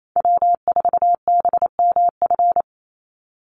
Morse code, heard as a keyed tone, about 28 wpm, 700 Hz. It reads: W4BMF
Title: USA Call Signs 28wpm